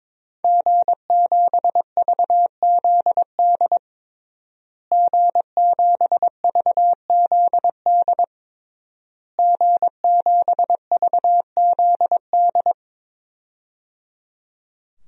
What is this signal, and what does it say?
Morse code, heard as a keyed tone, about 22 wpm, 700 Hz. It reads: G7VZD G7VZD G7VZD
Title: DXCC-100 Call Signs - 5 Characters - Repeated 3x - 22wpm